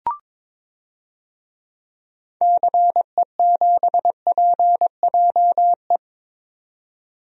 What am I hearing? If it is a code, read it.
Morse code, heard as a keyed tone, about 22 wpm, 700 Hz. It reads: CE7PJE
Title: DXCC-100 Call Signs - 6 Characters - 22wpm